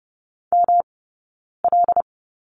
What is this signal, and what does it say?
Morse code, heard as a keyed tone, about 30 wpm, spaced as for 9 wpm, 700 Hz. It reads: ML
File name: Lesson 18d - Letter m - Two Characters 30wpm